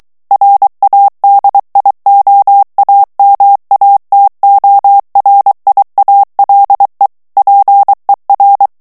Morse code, heard as a keyed tone, about 23 wpm, 800 Hz. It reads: RADIOAMATORIALE PER
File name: morse mp3 / ITALIEN texte clair 1 vitesse 22 mots minute